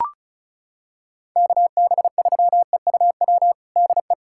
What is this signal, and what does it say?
Morse code, heard as a keyed tone, about 34 wpm, 700 Hz. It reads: KB3EUW DE
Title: USA Callsign DE Callsign - 6 Characters - 35wpm